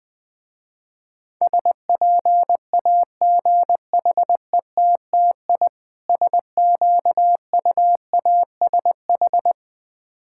Morse code, heard as a keyed tone, about 20 wpm, 700 Hz. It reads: SPAGHETTI SQUASH
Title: Difficult Words with Hints - 20wpm